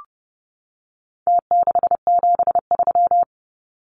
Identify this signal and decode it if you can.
Morse code, heard as a keyed tone, about 30 wpm, 700 Hz. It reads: T673
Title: Numbers - Contest Style - 4-Digit 30wpm